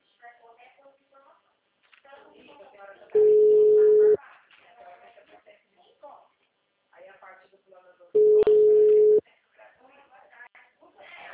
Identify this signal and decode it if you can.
Ringback tone (Europe, 425 Hz), with speech.